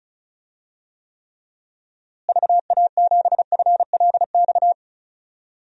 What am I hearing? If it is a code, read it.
Morse code, heard as a keyed tone, about 35 wpm, 700 Hz. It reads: VA7FLX
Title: DXCC-100 Call Signs - 6 Characters - Repeated 2x - 35wpm